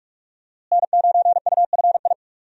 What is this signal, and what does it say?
Morse code, heard as a keyed tone, about 46 wpm, 700 Hz. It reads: N9UFI
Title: USA Call Signs - 5 Characters - Repeated 3x - 45wpm